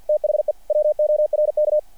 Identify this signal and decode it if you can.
Morse code, heard as a keyed tone, about 48 wpm, 600 Hz. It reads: THE WORK
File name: archived / CRSnet